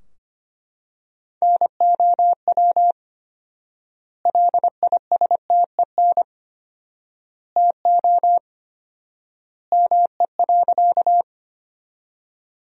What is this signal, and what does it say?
Morse code, heard as a keyed tone, about 25 wpm, 700 Hz. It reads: NOW LISTEN TO ME.
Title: 4x Word-Spacing - Sentences from Top 400 Words - 25wpm